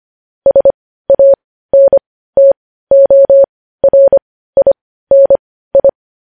Morse code, heard as a keyed tone, about 25 wpm, spaced as for 15 wpm, 550 Hz. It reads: SANTORINI
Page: News Headlines in Morse Code at 15 WPM